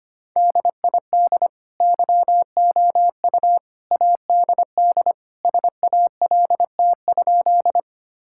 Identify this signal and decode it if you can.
Morse code, heard as a keyed tone, about 25 wpm, 700 Hz. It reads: DID YOU ADD SALT?
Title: Sentences from Top 1000 Words - Review Part 5 - 25wpm